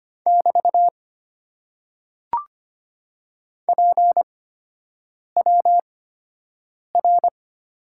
Morse code, heard as a keyed tone, about 25 wpm, spaced as for 7 wpm, 700 Hz. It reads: <BT> PWR
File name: Top 90 QSO Elements - 8x Character Spacing - 25wpm